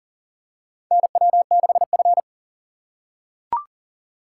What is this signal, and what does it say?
Morse code, heard as a keyed tone, about 40 wpm, 700 Hz. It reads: NW6F
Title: USA Call Signs - 4 Characters - 4x Character Spacing - 40wpm